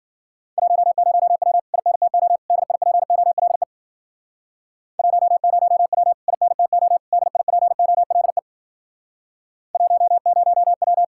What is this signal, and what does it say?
Morse code, heard as a keyed tone, about 53 wpm, 700 Hz. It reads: 10W INTO DIPOLE 10W INTO DIPOLE 10W